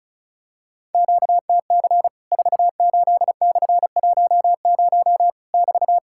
Morse code, heard as a keyed tone, about 36 wpm, 700 Hz. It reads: QTC 48/10 <BT>
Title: Worked All Europe DX Contest - QTC Exchange - 35wpm